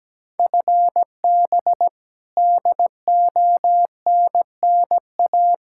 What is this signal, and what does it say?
Morse code, heard as a keyed tone, about 17 wpm, 700 Hz. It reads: FB DONNA
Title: Practice QSO - Exchange 3 - More Info - 17wpm